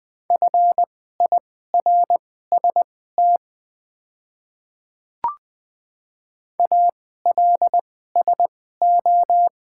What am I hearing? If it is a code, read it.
Morse code, heard as a keyed tone, about 20 wpm, spaced as for 14 wpm, 700 Hz. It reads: FIRST ALSO
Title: 2x Character Spacing - Top 200 Words - 20wpm